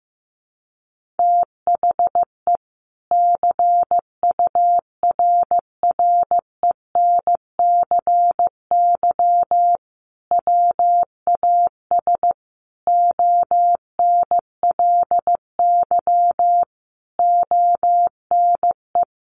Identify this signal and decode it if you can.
Morse code, heard as a keyed tone, about 15 wpm, 700 Hz. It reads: THE CURRENCY WAS ONLY ONE
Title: Sentences from Top 1500 Words 15wpm